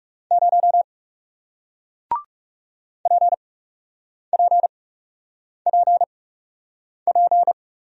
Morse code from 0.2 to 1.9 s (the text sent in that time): 0